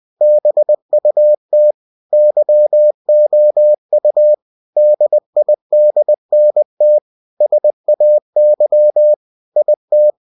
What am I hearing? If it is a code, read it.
Morse code, heard as a keyed tone, about 20 wpm, 600 Hz. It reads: BUT YOU DIDNT SAY IT